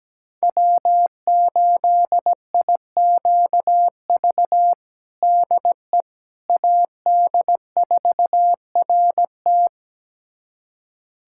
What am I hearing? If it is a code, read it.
Morse code, heard as a keyed tone, about 17 wpm, 700 Hz. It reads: W8IQV DE AD4RT
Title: USA Callsign DE Callsign 17wpm